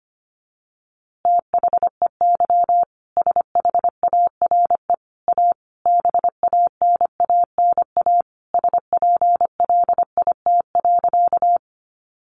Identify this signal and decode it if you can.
Morse code, heard as a keyed tone, about 25 wpm, 700 Hz. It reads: THEY SHARE A BANANA SPLIT.
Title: Easy Phrases 25wpm